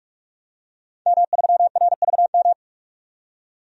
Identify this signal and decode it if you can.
Morse code, heard as a keyed tone, about 42 wpm, 700 Hz. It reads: M3RVM
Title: DXCC-100 Call Signs - 5 Characters - 45wpm